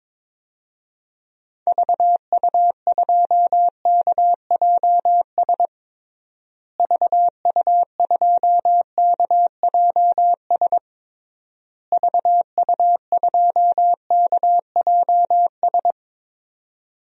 Morse code, heard as a keyed tone, about 22 wpm, 700 Hz. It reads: VU2KJS VU2KJS VU2KJS
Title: DXCC-100 Call Signs - 6 Characters - Repeated 3x - 22wpm